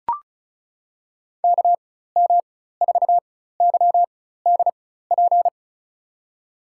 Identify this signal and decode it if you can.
Morse code, heard as a keyed tone, about 36 wpm, spaced as for 16 wpm, 700 Hz. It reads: KM4YDP
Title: USA Call Signs - 4x Character Spacing - 35wpm